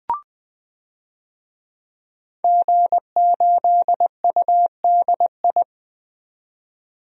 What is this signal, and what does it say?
Morse code, heard as a keyed tone, about 20 wpm, 700 Hz. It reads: G8UDI
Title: DXCC-100 Call Signs - 5 Characters - 20wpm